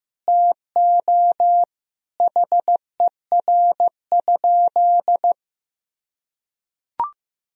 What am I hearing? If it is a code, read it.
Morse code, heard as a keyed tone, about 16 wpm, 700 Hz. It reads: TO HER?
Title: Sentences from Top 200 Words and Abbreviations 15wpm